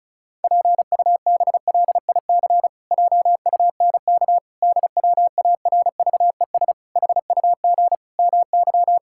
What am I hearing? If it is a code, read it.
Morse code, heard as a keyed tone, about 34 wpm, 700 Hz. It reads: PUBLIC JUNK DWARVES HUG MY
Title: Pangram Challenge v2 - 35wpm